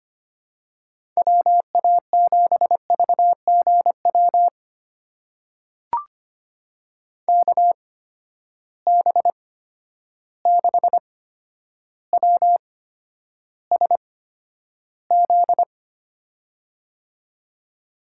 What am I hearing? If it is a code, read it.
Morse code, heard as a keyed tone, about 25 wpm, spaced as for 11 wpm, 700 Hz. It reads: WA7VGW KB6WSZ
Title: USA Call Signs - 8x Character Spacing - 25wpm